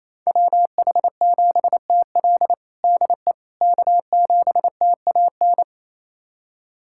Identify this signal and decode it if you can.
Morse code, heard as a keyed tone, about 28 wpm, 700 Hz. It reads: WH7TL DE K7TAN
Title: USA Callsign DE Callsign - 5 Characters - 28wpm